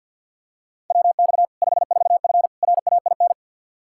Morse code, heard as a keyed tone, about 51 wpm, 700 Hz. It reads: WX 54F RAIN
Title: Practice QSO - Exchange 2 - Info - 50wpm